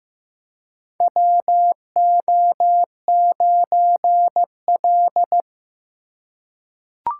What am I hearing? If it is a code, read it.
Morse code, heard as a keyed tone, about 15 wpm, 700 Hz. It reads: WO9L
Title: USA Call Signs - 4 Characters - 3x Character Spacing - 15wpm